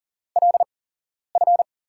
Morse code, heard as a keyed tone, about 39 wpm, spaced as for 11 wpm, 700 Hz. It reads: LF
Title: USA Call Signs - 6 Characters - 8x Character Spacing - 40wpm